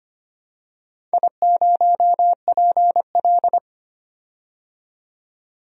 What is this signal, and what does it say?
Morse code, heard as a keyed tone, about 25 wpm, 700 Hz. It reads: I0PL
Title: DXCC-100 Call Signs - 4 Characters - 25wpm